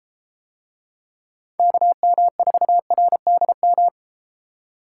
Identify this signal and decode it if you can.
Morse code, heard as a keyed tone, about 33 wpm, 700 Hz. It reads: KM4RDM